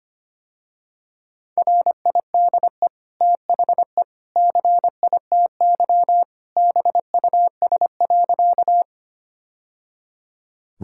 Morse code, heard as a keyed tone, about 25 wpm, 700 Hz. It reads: RIDE THE CITY BUS.
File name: Easy Sentences with Hints - 25wpm